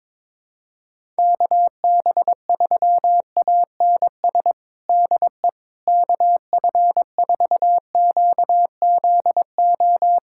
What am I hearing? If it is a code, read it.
Morse code, heard as a keyed tone, about 22 wpm, 700 Hz. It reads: KB3ANS DE KF4QZO
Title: USA Callsign DE Callsign - 6 Characters - 22wpm